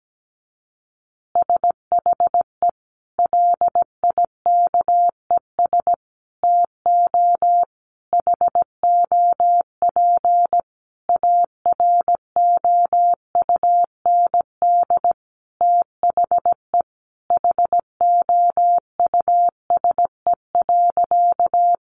Morse code, heard as a keyed tone, about 17 wpm, 700 Hz. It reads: SHE LIKES TO HOP AROUND THE HOUSE.
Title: Easy Phrases 17wpm